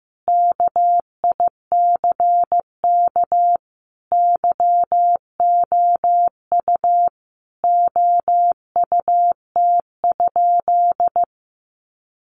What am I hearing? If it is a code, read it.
Morse code, heard as a keyed tone, about 15 wpm, 700 Hz. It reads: KICK YOU OUT?
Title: Sentences from Top 1300 Words 15wpm